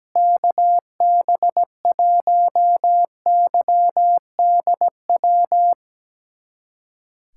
Morse code, heard as a keyed tone, about 17 wpm, 700 Hz. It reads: KB1YDW
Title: USA Callsign DE Callsign - 6 Characters - 17wpm